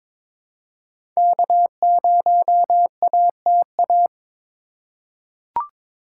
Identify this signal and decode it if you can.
Morse code, heard as a keyed tone, about 22 wpm, 700 Hz. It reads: K0ATA
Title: USA Call Signs - 5 Characters - 4x Character Spacing - 22wpm